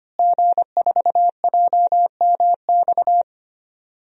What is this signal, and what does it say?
Morse code, heard as a keyed tone, about 25 wpm, 700 Hz. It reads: G4JMX
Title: DXCC-100 Call Signs - 5 Characters - Repeated 3x - 25wpm